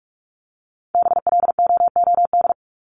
Morse code, heard as a keyed tone, about 44 wpm, 700 Hz. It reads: BLOOD
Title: Top 1001-1250 Words 45wpm